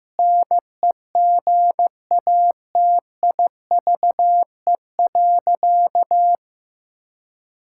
Morse code, heard as a keyed tone, about 15 wpm, 700 Hz. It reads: NEGATIVE.
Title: Sentences from Top 1000 Words - Review Part 6 - 15wpm